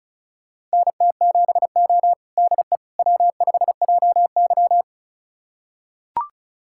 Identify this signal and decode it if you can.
Morse code, heard as a keyed tone, about 34 wpm, 700 Hz. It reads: NT7O DE W5JY
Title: USA Callsign DE Callsign - 4 Characters - 35wpm